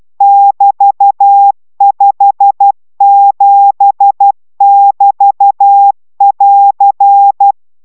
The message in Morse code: <BT>57<BT><AR>

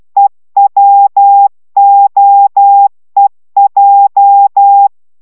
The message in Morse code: EWOEJ

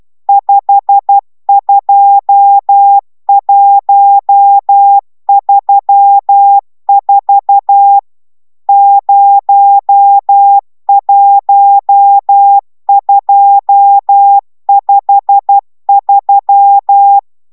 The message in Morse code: 52134 01253

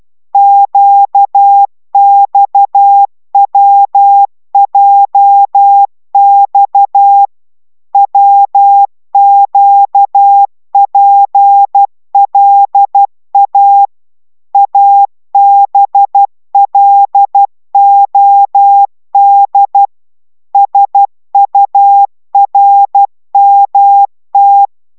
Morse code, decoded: QXWJX WQPLA ABLOD SURMT

